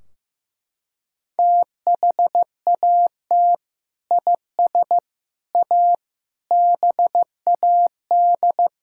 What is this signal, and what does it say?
Morse code, heard as a keyed tone, about 15 wpm, 700 Hz. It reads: THAT IS A BAD
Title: Sentences from Top 1600 Words 15wpm